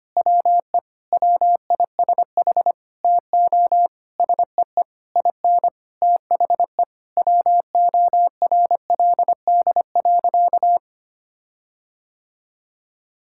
Morse code, heard as a keyed tone, about 25 wpm, 700 Hz. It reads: WE WISH TO SEE IN THE WORLD.
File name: practice / 25wpm / Sentences with 9-Word Phrases - 25wpm